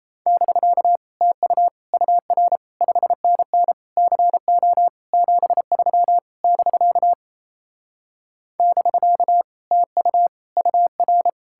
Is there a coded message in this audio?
Yes: Morse code from 0.3 to 11.3 s, heard as a keyed tone, about 37 wpm, 700 Hz. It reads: <BK> TU UR 5NN CO 73 <BK> <BK> TU UR